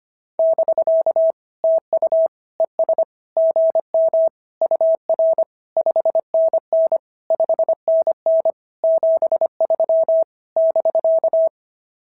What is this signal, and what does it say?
Morse code, heard as a keyed tone, about 22 wpm, 650 Hz. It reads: <BK> TU ES GM UR 5NN 5NN 73 <BK>
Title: SOTA QSO Practice 25wpm